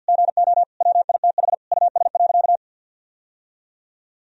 Morse code, heard as a keyed tone, about 52 wpm, 700 Hz. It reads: GO WITH US.
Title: Sentences from Top 1000 Words - Review Part 3 - 50wpm